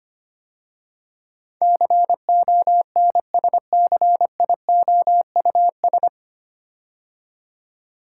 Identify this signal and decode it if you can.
Morse code, heard as a keyed tone, about 25 wpm, 690 Hz. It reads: CONSCIOUS
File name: practice / 25wpm / 9 Letter Words - First 3 Letters Given - 25wpm